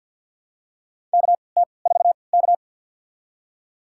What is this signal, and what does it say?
Morse code, heard as a keyed tone, about 48 wpm, spaced as for 28 wpm, 700 Hz. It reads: KT4K
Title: USA Call Signs - 4 Characters - 3x Character Spacing - 50wpm